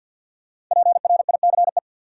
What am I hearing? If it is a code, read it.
Morse code, heard as a keyed tone, about 53 wpm, 700 Hz. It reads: PRICE
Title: Five Letter Words - 50wpm